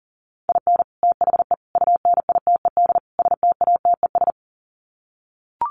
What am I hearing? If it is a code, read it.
Morse code, heard as a keyed tone, about 37 wpm, 700 Hz. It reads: IN THE UNITED STATES
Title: Sets of 4 Words 40wpm